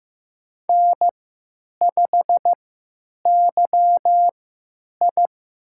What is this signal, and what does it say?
Morse code, heard as a keyed tone, about 15 wpm, spaced as for 9 wpm, 700 Hz. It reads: N5YI
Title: USA Call Signs - 4 Characters - 3x Character Spacing - 15wpm